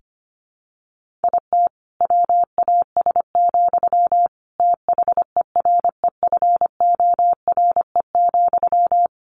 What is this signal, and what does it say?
Morse code, heard as a keyed tone, about 25 wpm, 700 Hz. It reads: IT WAS, THEREFORE,
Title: Sentences from Top 1500 Words 25wpm